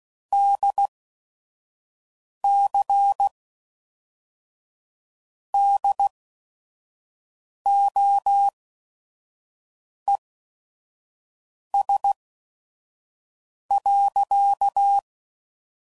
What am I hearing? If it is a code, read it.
Morse code, heard as a keyed tone, about 16 wpm, spaced as for 5 wpm, 800 Hz. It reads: DC DOES.
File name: CW Code Groups @5WPM